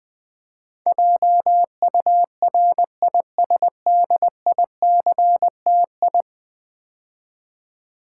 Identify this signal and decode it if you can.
Morse code, heard as a keyed tone, about 20 wpm, 700 Hz. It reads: JURISDICTI
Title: Build A Word - 5 Letters At A Time - 12-Letter Words - 20wpm